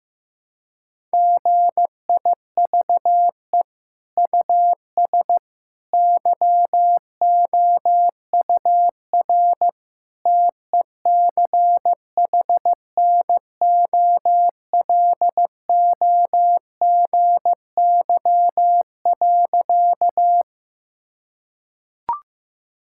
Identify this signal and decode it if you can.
Morse code, heard as a keyed tone, about 15 wpm, 700 Hz. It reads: GIVE US YOUR TECHNOLOGY.
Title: Sentences from Top 1000 Words - Review Part 5 - 15wpm